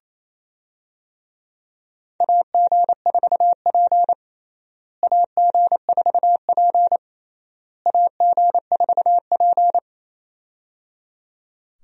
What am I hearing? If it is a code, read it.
Morse code, heard as a keyed tone, about 28 wpm, 700 Hz. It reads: AG4P AG4P AG4P
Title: USA Call Signs - 4 Characters - Repeated 3x - 28wpm